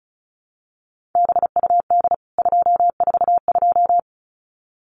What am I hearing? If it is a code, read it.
Morse code, heard as a keyed tone, about 34 wpm, 700 Hz. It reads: BUD 242
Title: CWT Exchange 35wpm